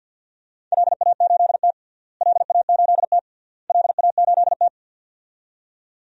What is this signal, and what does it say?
Morse code, heard as a keyed tone, about 53 wpm, 700 Hz. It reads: LA8T LA8T LA8T
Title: DXCC-100 Call Signs - 4 Characters - Repeated 3x - 50wpm